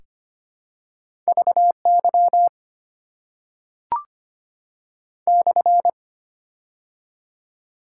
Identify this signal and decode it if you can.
Morse code, heard as a keyed tone, about 25 wpm, 700 Hz. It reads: VY /
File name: Top 90 QSO Elements - 6x Character Spacing - 25wpm